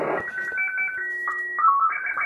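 SSTV (slow-scan television), Scottie 1.
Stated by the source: Pirate radio station signing off with SSTV.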